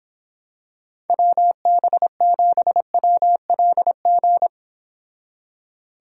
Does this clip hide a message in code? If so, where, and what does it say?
Yes: Morse code from 1.1 to 4.5 s, heard as a keyed tone, about 26 wpm, 700 Hz. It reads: WB7WLG